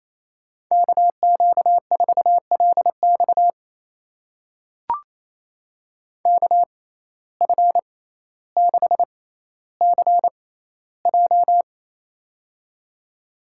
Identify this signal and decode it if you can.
Morse code, heard as a keyed tone, about 28 wpm, spaced as for 14 wpm, 700 Hz. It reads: KQ4LX KF6CJ
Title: USA Call Signs - 5 Characters - 6x Character Spacing - 28wpm